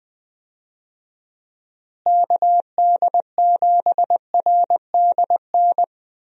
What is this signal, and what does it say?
Morse code, heard as a keyed tone, about 20 wpm, 700 Hz. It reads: KD7RDN
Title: USA Call Signs - 6 Characters - Repeated 3x - 20wpm